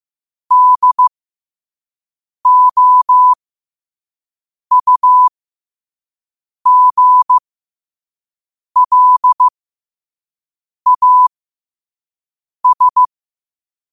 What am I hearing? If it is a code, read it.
Morse code, heard as a keyed tone, about 15 wpm, spaced as for 5 wpm, 1000 Hz. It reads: DOUGLAS
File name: qotdfiles / 5wpm